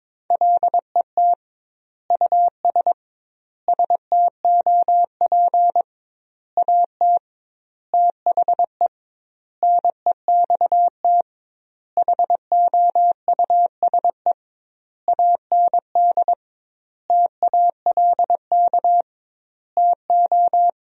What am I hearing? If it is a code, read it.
Morse code, heard as a keyed tone, about 22 wpm, 700 Hz. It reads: LET US STOP AT THE NEXT HOUSE AND TALK TO
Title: 2x Word-Spacing - Sentences from Top 400 Words - 22wpm